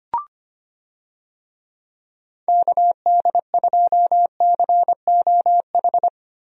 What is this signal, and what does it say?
Morse code, heard as a keyed tone, about 25 wpm, 700 Hz. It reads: KD2COH
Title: USA Call Signs - 6 Characters - 25wpm